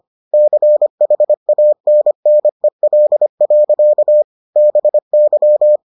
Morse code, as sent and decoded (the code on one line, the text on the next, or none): -.-. .... .- -. -. .-.. .-.-.- / -... -.--
CHANNEL. BY